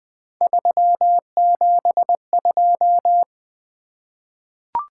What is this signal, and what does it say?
Morse code, heard as a keyed tone, about 20 wpm, 700 Hz. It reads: 372
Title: Three Letter-Number Combinations - 20wpm